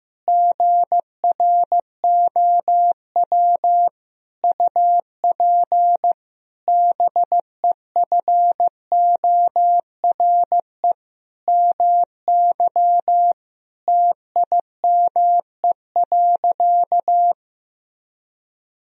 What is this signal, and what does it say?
Morse code, heard as a keyed tone, about 15 wpm, 700 Hz. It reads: GROW UP BEFORE MY TIME.
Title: Sentences from Top 1000 Words - Review Part 1 - 15wpm